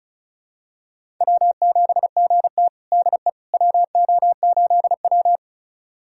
Morse code, heard as a keyed tone, about 36 wpm, 700 Hz. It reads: W7GT DE WO8W
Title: USA Callsign DE Callsign - 4 Characters - 35wpm